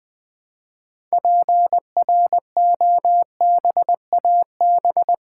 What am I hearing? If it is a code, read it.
Morse code, heard as a keyed tone, about 20 wpm, 700 Hz. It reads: PROBAB
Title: Build A Word - 3 Letters At A Time - 8-Letter Words - 20wpm